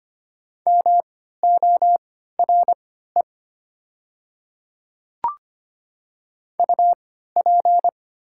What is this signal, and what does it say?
Morse code, heard as a keyed tone, about 25 wpm, spaced as for 14 wpm, 700 Hz. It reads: MORE UP
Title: 3x Character Spacing - Top 100 Words - 25wpm